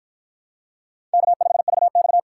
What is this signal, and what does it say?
Morse code, heard as a keyed tone, about 50 wpm, 700 Hz. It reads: K5VX